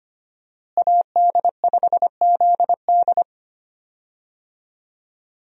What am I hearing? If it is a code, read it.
Morse code, heard as a keyed tone, about 25 wpm, 700 Hz. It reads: AD5ZD
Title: USA Call Signs - 5 Characters - Repeated 3x - 25wpm